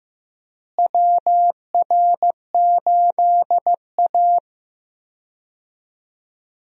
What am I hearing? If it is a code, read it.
Morse code, heard as a keyed tone, about 15 wpm, 700 Hz. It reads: WR8A
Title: USA Call Signs - 4 Characters - Repeated 2x - 15wpm